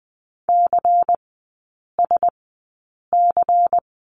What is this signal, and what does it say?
Morse code, heard as a keyed tone, about 20 wpm, spaced as for 8 wpm, 690 Hz. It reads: CSC
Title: 093 Lesson 16e - Letter c - Three Characters 20wpm